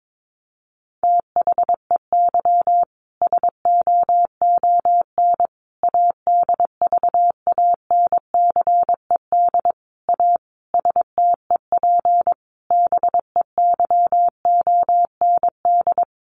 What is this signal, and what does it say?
Morse code, heard as a keyed tone, about 22 wpm, 700 Hz. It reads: THEY SOON ADVANCED A STEP BEYOND
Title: Sentences from Top 1200 Words 22wpm